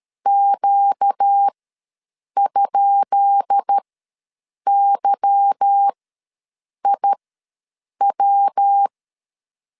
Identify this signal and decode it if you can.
Morse code, heard as a keyed tone, about 13 wpm, spaced as for 7 wpm, 800 Hz. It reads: Q?YIW
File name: LEZIONI / 10 SIGNS / 10B INT ws 12wpm 3527